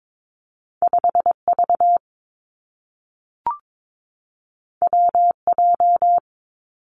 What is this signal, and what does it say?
Morse code, heard as a keyed tone, about 22 wpm, 700 Hz. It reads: 5 V W J
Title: Two Challenging Letter-Number Combinations 22wpm